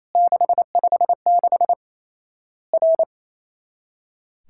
Morse code, heard as a keyed tone, about 28 wpm, 700 Hz. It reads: 656 R
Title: Worked All Europe DX Contest - QTC Exchange - 28wpm